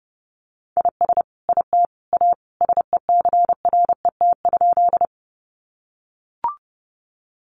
Morse code, heard as a keyed tone, about 30 wpm, 700 Hz. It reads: IS IT A SECRET?
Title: Sentences from Top 1200 Words 30wpm